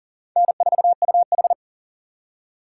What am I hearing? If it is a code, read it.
Morse code, heard as a keyed tone, about 51 wpm, 700 Hz. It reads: N4UH